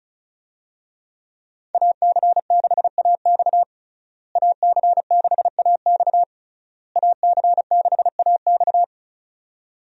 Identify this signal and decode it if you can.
Morse code, heard as a keyed tone, about 36 wpm, 700 Hz. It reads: AC6AX AC6AX AC6AX
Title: USA Call Signs - 5 Characters - Repeated 3x - 35wpm